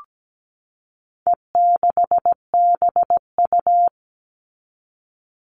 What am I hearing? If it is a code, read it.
Morse code, heard as a keyed tone, about 17 wpm, 700 Hz. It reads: E6BU
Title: All Cut Numbers - 4-Digit 17wpm